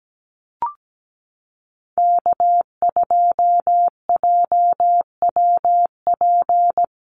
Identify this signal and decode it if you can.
Morse code, heard as a keyed tone, about 17 wpm, 700 Hz. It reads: K2JWP
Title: USA Call Signs Repeated 3x 17wpm